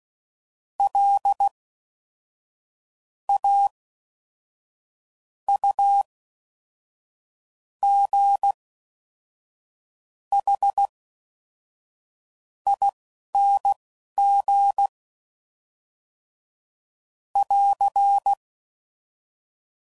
Morse code, heard as a keyed tone, about 16 wpm, spaced as for 5 wpm, 800 Hz. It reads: LAUGHING <AR>